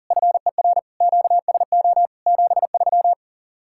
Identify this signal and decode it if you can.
Morse code, heard as a keyed tone, about 48 wpm, 700 Hz. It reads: FER QSO 73
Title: Speed-Racing Common QSO Phrases Encore - 40wpm